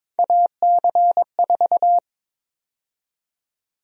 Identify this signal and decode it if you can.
Morse code, heard as a keyed tone, about 21 wpm, 700 Hz. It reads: AC4
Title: USA Call Sign Prefixes - Repeated 2x - 22wpm